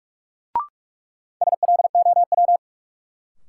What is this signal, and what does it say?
Morse code, heard as a keyed tone, about 46 wpm, 700 Hz. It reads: SLOW
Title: Sentences from Top 400 Words - New Word Drill - 45wpm